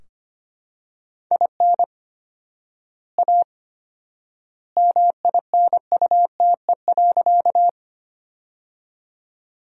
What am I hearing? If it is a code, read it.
Morse code, heard as a keyed tone, about 25 wpm, 700 Hz. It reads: IN A MINUTE.
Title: 4x Word-Spacing - Sentences from Top 500 Words - 25wpm